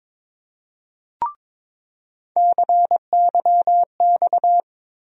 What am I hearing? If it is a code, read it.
Morse code, heard as a keyed tone, about 22 wpm, 700 Hz. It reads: CYX
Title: USA Call Sign Postfixes - 3 Letters 22wpm